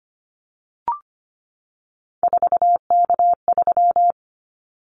Morse code, heard as a keyed tone, about 25 wpm, 700 Hz. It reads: 4K3
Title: Callsign Prefixes for Top 100 DXCC - 25wpm